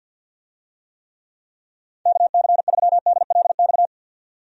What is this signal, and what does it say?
Morse code, heard as a keyed tone, about 47 wpm, 700 Hz. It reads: KC3DLX
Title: USA Call Signs - 6 Characters - 50wpm